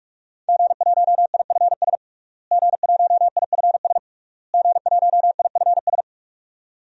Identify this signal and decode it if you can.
Morse code, heard as a keyed tone, about 46 wpm, 700 Hz. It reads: G1IFS G1IFS G1IFS
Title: DXCC-100 Call Signs - 5 Characters - Repeated 3x - 45wpm